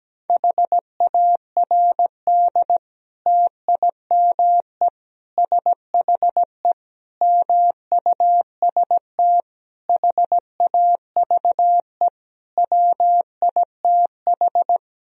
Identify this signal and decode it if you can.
Morse code, heard as a keyed tone, about 17 wpm, 700 Hz. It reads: HARD TIME SHE MUST HAVE WITH
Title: Sentences from Top 1000 Words - Review Part 1 - 17wpm